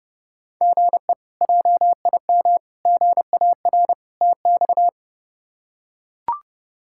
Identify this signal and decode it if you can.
Morse code, heard as a keyed tone, about 32 wpm, 700 Hz. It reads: GE JIM GAR TX
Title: K1USN SST QSO Practice - 30wpm